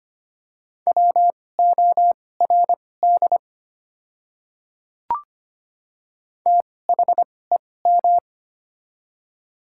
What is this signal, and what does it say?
Morse code, heard as a keyed tone, about 25 wpm, spaced as for 18 wpm, 700 Hz. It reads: WORD THEM
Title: 2x Character Spacing - Top 100 Words - 25wpm